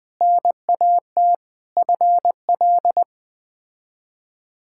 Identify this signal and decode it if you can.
Morse code, heard as a keyed tone, about 20 wpm, 700 Hz. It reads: NAT FL
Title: K1USN SST QSO Practice - 20wpm